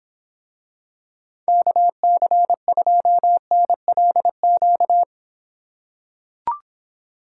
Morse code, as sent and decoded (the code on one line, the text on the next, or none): -.- -.-. ..--- -. .-.. --.-
KC2NLQ